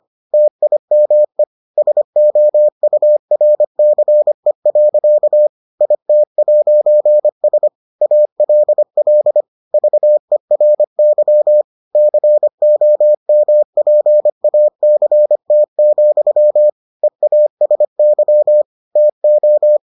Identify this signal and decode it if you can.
Morse code, heard as a keyed tone, about 25 wpm, 600 Hz. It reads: TIME SOURCE. IT'S ALL VERY COMPACT, EASY TO